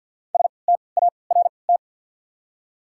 Morse code, heard as a keyed tone, about 54 wpm, spaced as for 29 wpm, 690 Hz. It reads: START
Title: 3x Character Spacing - Top 300 Words - 50wpm